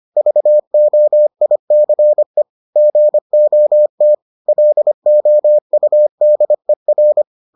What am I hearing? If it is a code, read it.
Morse code, heard as a keyed tone, about 25 wpm, 600 Hz. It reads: VOICE GOT LOUDER